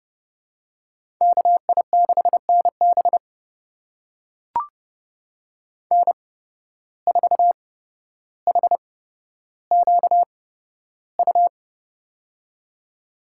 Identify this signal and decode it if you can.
Morse code, heard as a keyed tone, about 30 wpm, spaced as for 13 wpm, 700 Hz. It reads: KI6NB N4HQU
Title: USA Call Signs - 5 Characters - 8x Character Spacing - 30wpm